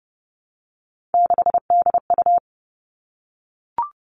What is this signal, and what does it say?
Morse code, heard as a keyed tone, about 30 wpm, 700 Hz. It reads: 6DU
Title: All Cut Numbers - 3-Digit 30wpm